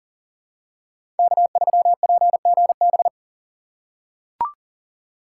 Morse code, heard as a keyed tone, about 40 wpm, 700 Hz. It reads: K3PGB
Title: USA Call Signs - 5 Characters - 6x Character Spacing - 40wpm